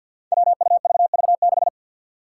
Morse code, heard as a keyed tone, about 51 wpm, 700 Hz. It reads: WUVV6
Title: Five Challenging Letter-Number Combinations 50wpm